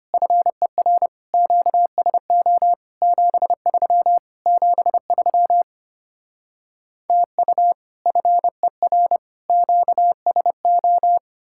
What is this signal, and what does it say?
Morse code, heard as a keyed tone, about 33 wpm, 700 Hz. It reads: FER QSO 73 73 TU FER QSO